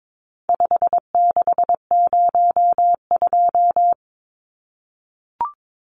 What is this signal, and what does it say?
Morse code, heard as a keyed tone, about 22 wpm, 700 Hz. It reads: 5602
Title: Numbers - 4-Digit 22wpm